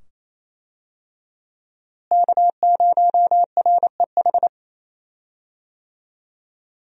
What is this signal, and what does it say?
Morse code, heard as a keyed tone, about 28 wpm, 700 Hz. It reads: K0REH
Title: USA Call Signs - 5 Characters - Repeated 2x - 28wpm